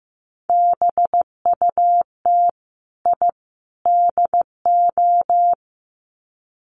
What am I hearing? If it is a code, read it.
Morse code, heard as a keyed tone, about 15 wpm, 700 Hz. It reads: BUT I DO
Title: Sets of 3 Words 15wpm